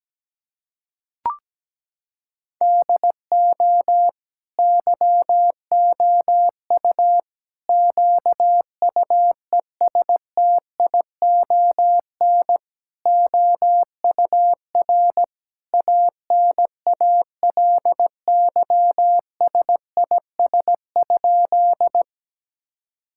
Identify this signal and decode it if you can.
Morse code, heard as a keyed tone, about 17 wpm, 700 Hz. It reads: DO YOU QUESTION OUR ANALYSIS?
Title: Sentences from Top 1000 Words - Review Part 5 - 17wpm